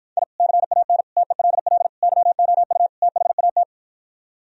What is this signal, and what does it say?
Morse code, heard as a keyed tone, about 52 wpm, 700 Hz. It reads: I CAN TELL YOU THAT